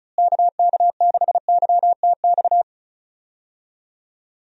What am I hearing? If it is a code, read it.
Morse code, heard as a keyed tone, about 36 wpm, 700 Hz. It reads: KK6YTX
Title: USA Call Signs - 6 Characters - Repeated 2x - 35wpm